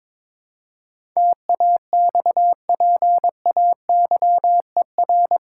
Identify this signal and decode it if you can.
Morse code, heard as a keyed tone, about 21 wpm, 700 Hz. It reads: TAXPAYER
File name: practice / 22wpm / Eight Letter Words - 22wpm